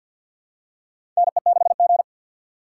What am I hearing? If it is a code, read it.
Morse code, heard as a keyed tone, about 52 wpm, 700 Hz. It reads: NE6G
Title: USA Call Signs - 4 Characters - 3x Character Spacing - 50wpm